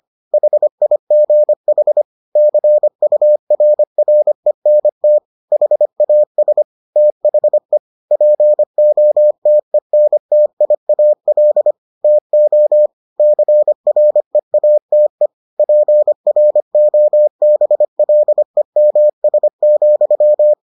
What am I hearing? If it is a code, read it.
Morse code, heard as a keyed tone, about 25 wpm, 600 Hz. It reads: HIGH CURRENT HAS THE POTENTIAL TO CREATE PROBLEMS,